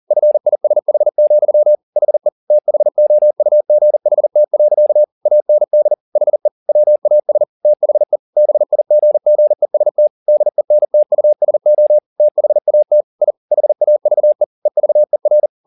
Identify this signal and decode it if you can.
Morse code, heard as a keyed tone, about 40 wpm, 600 Hz. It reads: FISH, HE THOUGHT. AND HE WAS THE BIGGEST DENTUSO THAT I HAVE EVER
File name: the old man and the sea 2 40wpm